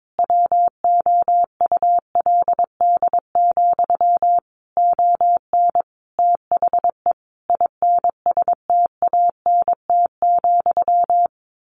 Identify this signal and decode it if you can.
Morse code, heard as a keyed tone, about 22 wpm, 700 Hz. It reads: WOULD, ON THE INSTANT,
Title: A Sound Of Thunder 22wpm